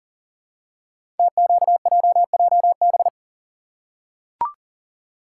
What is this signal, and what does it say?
Morse code, heard as a keyed tone, about 38 wpm, 700 Hz. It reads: TQJJB
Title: Code Groups - 5 Letter:Numbers - 40wpm